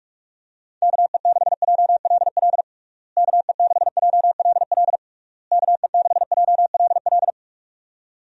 Morse code, heard as a keyed tone, about 41 wpm, 700 Hz. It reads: KE6JLL KE6JLL KE6JLL
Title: USA Call Signs - 6 Characters - Repeated 3x - 45wpm